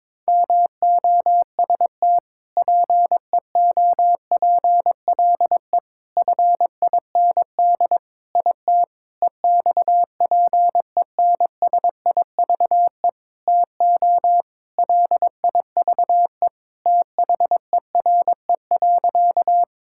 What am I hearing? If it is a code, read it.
Morse code, heard as a keyed tone, about 22 wpm, 700 Hz. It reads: MOST PEOPLE FIND IT EXPENSIVE TO LIVE THERE.
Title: Sentences from Top 1000 Words - Review Part 6 - 22wpm